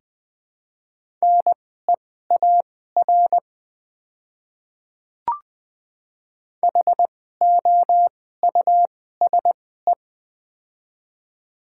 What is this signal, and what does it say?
Morse code, heard as a keyed tone, about 20 wpm, spaced as for 15 wpm, 700 Hz. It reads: NEAR HOUSE
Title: 2x Character Spacing - Top 200 Words - 20wpm